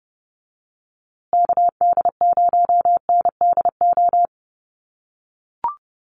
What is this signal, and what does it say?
Morse code, heard as a keyed tone, about 30 wpm, 700 Hz. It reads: KD0NDO
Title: USA Call Signs Repeated 3x 30wpm